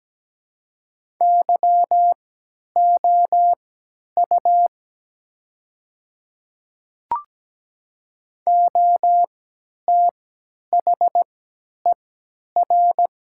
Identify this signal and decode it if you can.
Morse code, heard as a keyed tone, about 17 wpm, spaced as for 10 wpm, 700 Hz. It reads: YOU OTHER
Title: 3x Character Spacing - Top 100 Words - 17wpm